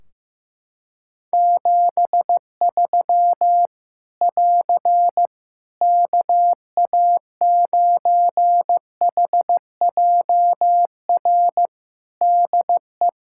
Morse code, heard as a keyed tone, about 16 wpm, 700 Hz. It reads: 73 <AR> KA9HJR DE
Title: Practice QSO - Exchange 4 - Ending - 15wpm